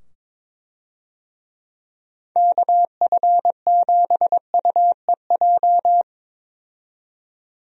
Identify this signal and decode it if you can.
Morse code, heard as a keyed tone, about 22 wpm, 700 Hz. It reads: KF7UEJ
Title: USA Call Signs - 6 Characters - 22wpm